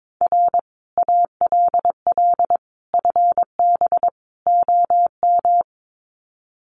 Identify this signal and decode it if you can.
Morse code, heard as a keyed tone, about 22 wpm, 700 Hz. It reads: R ALL FB OM
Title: Common QSO Phrases 22wpm